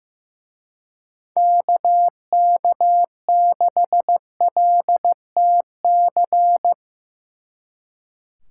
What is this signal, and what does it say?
Morse code, heard as a keyed tone, about 17 wpm, 700 Hz. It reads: KK6LTC
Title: Speed Racing - USA 2x3 Call Signs 15wpm